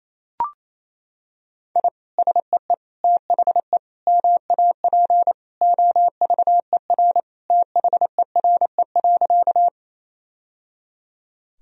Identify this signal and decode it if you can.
Morse code, heard as a keyed tone, about 28 wpm, 700 Hz. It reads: I SEE THE MAP OVER THERE.
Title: Sentences from Top 1000 Words - Review Part 2 - 28wpm